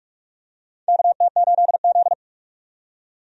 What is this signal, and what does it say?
Morse code, heard as a keyed tone, about 44 wpm, 700 Hz. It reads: KT8Z
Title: USA Call Signs - 4 Characters - 45wpm